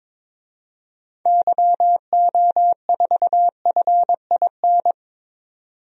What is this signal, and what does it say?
Morse code, heard as a keyed tone, about 22 wpm, 700 Hz. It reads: YO4FIN
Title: DXCC-100 Call Signs - 6 Characters - 22wpm